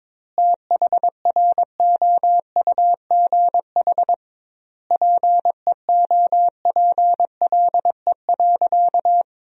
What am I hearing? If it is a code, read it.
Morse code, heard as a keyed tone, about 22 wpm, 700 Hz. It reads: THROUGH PEOPLE.
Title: 2x Word-Spacing - Sentences from Top 200 Words - 22wpm